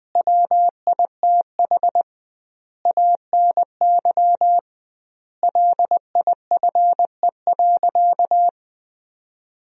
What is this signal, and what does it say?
Morse code, heard as a keyed tone, about 20 wpm, 700 Hz. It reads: WITH ANY LIFE.
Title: 2x Word-Spacing - Sentences from Top 300 Words - 20wpm